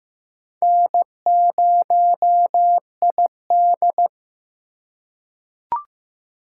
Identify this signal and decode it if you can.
Morse code, heard as a keyed tone, about 15 wpm, 700 Hz. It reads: N0ID
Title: USA Call Signs - 4 Characters - 2x Character Spacing - 15wpm